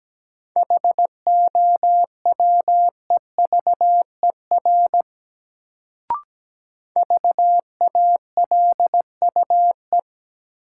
Morse code, heard as a keyed tone, about 17 wpm, 700 Hz. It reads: HOWEVER VALUE
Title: Sentences from Top 600 Words - New Word Drill - 17wpm